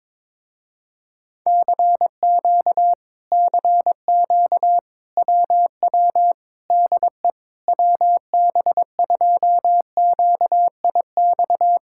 Morse code, heard as a keyed tone, about 22 wpm, 700 Hz. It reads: CQ CQ WW DE WB2QIX
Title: Speed-Racing Common QSO Phrases - 22wpm